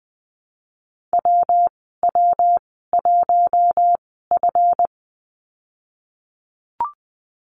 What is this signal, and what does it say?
Morse code, heard as a keyed tone, about 20 wpm, spaced as for 15 wpm, 700 Hz. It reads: WW1F
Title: 154 Lesson 25g - Letter p - Call Signs 20wpm